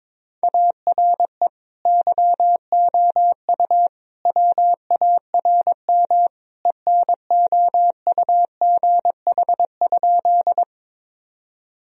Morse code, heard as a keyed tone, about 22 wpm, 700 Hz. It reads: ARE YOU WARM ENOUGH?
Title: Sentences from Top 1000 Words - Review Part 2 - 22wpm